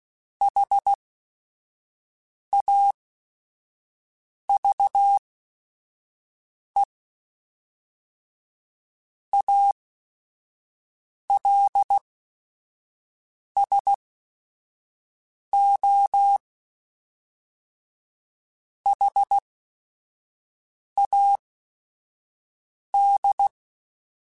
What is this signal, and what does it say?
Morse code, heard as a keyed tone, about 16 wpm, spaced as for 5 wpm, 800 Hz. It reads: HAVE ALSO HAD
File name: Weather Sets / Weather Sets-mp3s / Weather 5 5 @5